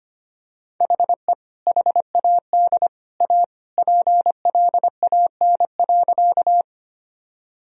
Morse code, heard as a keyed tone, about 25 wpm, 700 Hz. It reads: HE HAD A PLAN.
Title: Sentences from Top 1000 Words - Review Part 2 - 25wpm